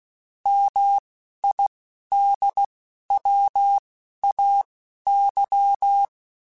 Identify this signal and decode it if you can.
Morse code, heard as a keyed tone, about 16 wpm, spaced as for 11 wpm, 800 Hz. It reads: MIDWAY